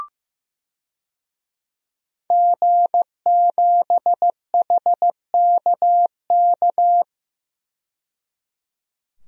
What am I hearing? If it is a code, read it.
Morse code, heard as a keyed tone, about 15 wpm, 700 Hz. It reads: G7HKK
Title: DXCC-100 Call Signs - 5 Characters - 15wpm